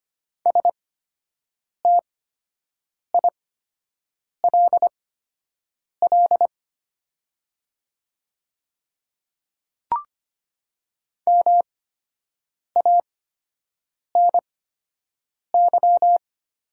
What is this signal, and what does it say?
Morse code, heard as a keyed tone, about 25 wpm, spaced as for 7 wpm, 700 Hz. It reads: STILL MANY
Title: 8x Character Spacing - Top 300 Words - 25wpm